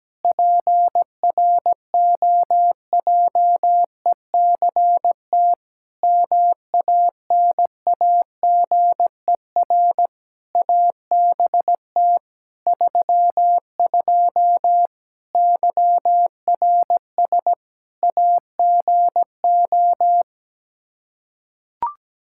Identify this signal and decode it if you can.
Morse code, heard as a keyed tone, about 17 wpm, 700 Hz. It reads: PROJECT MANAGER ABT 32 YRS AGO
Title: Practice QSO - Exchange 3 - More Info - 17wpm